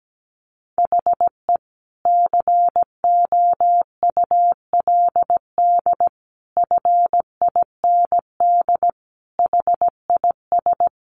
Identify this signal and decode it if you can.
Morse code, heard as a keyed tone, about 17 wpm, 700 Hz. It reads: HE COULD FIND HIS
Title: Sentences from Top 1400 Words 17wpm